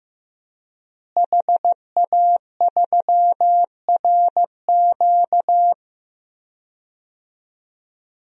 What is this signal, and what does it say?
Morse code, heard as a keyed tone, about 15 wpm, 700 Hz. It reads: HA3RQ
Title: DXCC-100 Call Signs - 5 Characters - Repeated 2x - 15wpm